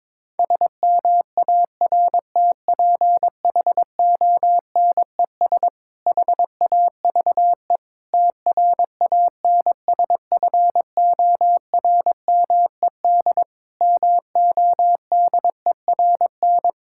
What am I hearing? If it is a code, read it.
Morse code, heard as a keyed tone, about 22 wpm, 700 Hz. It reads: SMARTPHONES HAVE TRANSFORMED MODERN
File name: Difficult Sentences with Hints - 22wpm